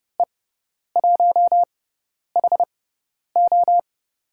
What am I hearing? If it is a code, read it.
Morse code, heard as a keyed tone, about 30 wpm, spaced as for 10 wpm, 700 Hz. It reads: E1HO
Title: USA Call Signs - 5 Characters - 6x Character Spacing - 30wpm